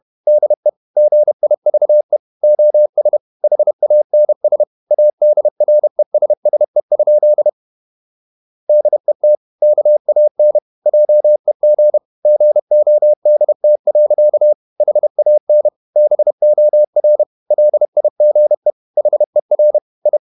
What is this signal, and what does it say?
Morse code, heard as a keyed tone, about 31 wpm, 600 Hz. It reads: DE GIVE OS HANS ADRESSE? DET KAN JEG GODT. HAN BOR LIGE HER I
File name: Ferie QRQ 31wpm